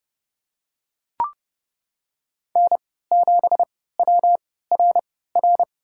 Morse code, heard as a keyed tone, about 31 wpm, spaced as for 17 wpm, 700 Hz. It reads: N7WRR
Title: USA Call Signs - 5 Characters - 3x Character Spacing - 30wpm